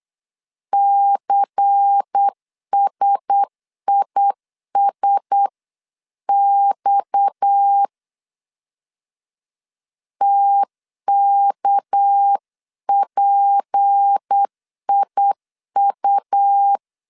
Morse code, heard as a keyed tone, about 8 wpm, 800 Hz. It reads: CSISX TKPIU
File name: CW / DET / 1 6 1500C S B 8wpm 5142